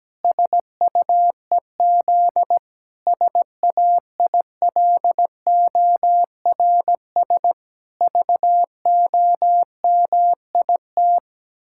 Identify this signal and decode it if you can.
Morse code, heard as a keyed tone, about 17 wpm, 700 Hz. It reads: SUEZ SAILORS VOMIT
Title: Pangram Challenge v2 - 17wpm